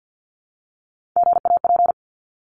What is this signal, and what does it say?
Morse code, heard as a keyed tone, about 50 wpm, 700 Hz. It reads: GAP
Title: Top 1501-1750 Words 50wpm